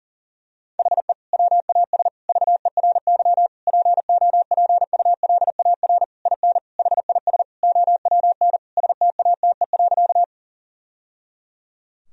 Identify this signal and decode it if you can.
Morse code, heard as a keyed tone, about 37 wpm, 700 Hz. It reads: HE WAS VERY POPULAR IN HIS OWN STATE.
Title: Sentences from Top 1000 Words - Review Part 3 - 40wpm